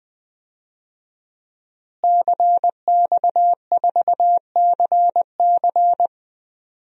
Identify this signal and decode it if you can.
Morse code, heard as a keyed tone, about 21 wpm, 700 Hz. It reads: CX4CC
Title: DXCC-100 Call Signs - 5 Characters - Repeated 2x - 20wpm